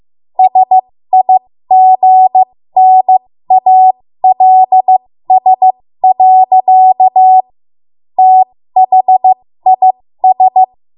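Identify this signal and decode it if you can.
Morse code, heard as a keyed tone, about 15 wpm, spaced as for 13 wpm, 750 Hz. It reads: SIGNALS. THIS
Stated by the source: 13 wpm Morse code practice archive transmission.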